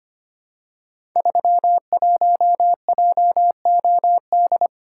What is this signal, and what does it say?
Morse code, heard as a keyed tone, about 25 wpm, 700 Hz. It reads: 31JOD
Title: Code Groups - 5 Letter:Numbers - 25wpm